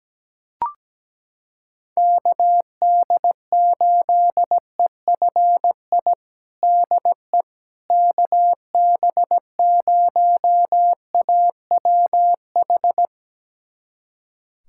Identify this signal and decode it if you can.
Morse code, heard as a keyed tone, about 17 wpm, 700 Hz. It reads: KD8EFI DE KB0AWH
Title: USA Callsign DE Callsign - 6 Characters - 17wpm